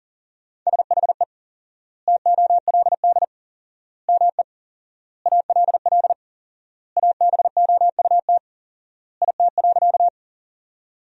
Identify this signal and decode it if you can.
Morse code, heard as a keyed tone, about 40 wpm, 700 Hz. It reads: SHE TOLD ME ALL ABOUT IT.
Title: 4x Word-Spacing - Sentences from Top 400 Words - 40wpm